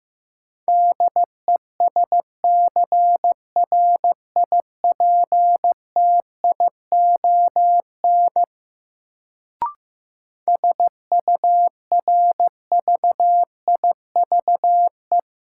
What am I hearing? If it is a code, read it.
Morse code, heard as a keyed tone, about 15 wpm, 700 Hz. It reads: DESCRIPTION SURVIVE
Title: Sentences from Top 1300 Words - New Word Drill Alt - 15wpm